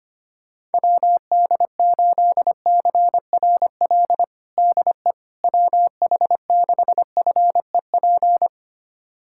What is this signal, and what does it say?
Morse code, heard as a keyed tone, about 25 wpm, 700 Hz. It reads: WD8CRL DE WH6FEP
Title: USA Callsign DE Callsign - 6 Characters - 25wpm